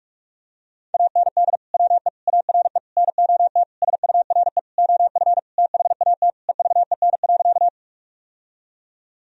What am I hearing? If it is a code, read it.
Morse code, heard as a keyed tone, about 41 wpm, 700 Hz. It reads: AND WE ARE NOT SURE OF THAT EVEN.
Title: Sentences from Top 1000 Words - Review Part 1 - 45wpm